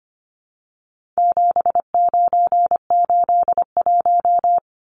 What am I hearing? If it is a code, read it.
Morse code, heard as a keyed tone, about 25 wpm, 700 Hz. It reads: 7981
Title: Numbers - 4-Digit 25wpm